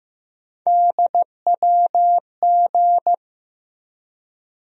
Morse code, heard as a keyed tone, about 15 wpm, 700 Hz. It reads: DWG
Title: USA Call Sign Postfixes - 3 Letters 15wpm